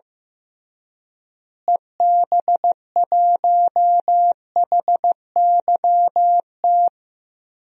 Morse code, heard as a keyed tone, about 15 wpm, 700 Hz. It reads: EB1HYT
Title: DXCC-100 Call Signs - 6 Characters - Repeated 3x - 15wpm